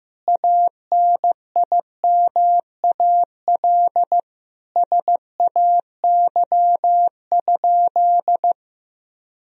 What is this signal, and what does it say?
Morse code, heard as a keyed tone, about 15 wpm, 700 Hz. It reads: ANIMAL SAY?
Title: Sentences from Top 1000 Words - Review Part 1 - 15wpm